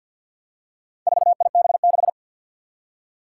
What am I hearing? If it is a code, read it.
Morse code, heard as a keyed tone, about 47 wpm, 700 Hz. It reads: 4IB6
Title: Four Challenging Letter-Number Combinations 50wpm